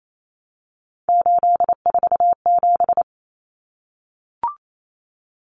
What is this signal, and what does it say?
Morse code, heard as a keyed tone, about 28 wpm, 700 Hz. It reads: 847
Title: Numbers - 3-Digit 28wpm